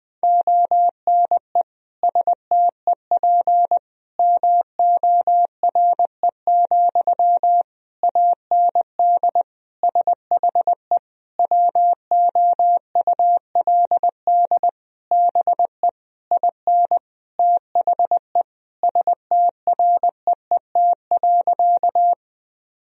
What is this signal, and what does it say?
Morse code, heard as a keyed tone, about 20 wpm, 700 Hz. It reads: ONE STEP MORE, AND SHE WOULD BE IN THE STREET.
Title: Sentences from Top 1000 Words - Review Part 3 - 20wpm